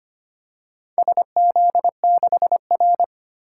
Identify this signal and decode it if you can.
Morse code, heard as a keyed tone, about 25 wpm, 700 Hz. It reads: SZ6R
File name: practice / 25wpm / Four Letter-Number Combinations - 25wpm